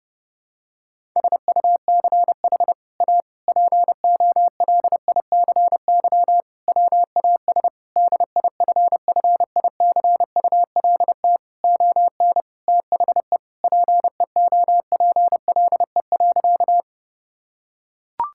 Morse code, heard as a keyed tone, about 30 wpm, 700 Hz. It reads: SUCH A POLICY WAS DIFFICULT ON THE PEOPLE.
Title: Sentences from Top 1000 Words - Review Part 5 - 30wpm